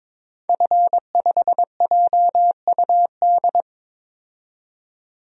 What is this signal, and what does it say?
Morse code, heard as a keyed tone, about 22 wpm, 700 Hz. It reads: F5JUD
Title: DXCC-100 Call Signs - 5 Characters - 22wpm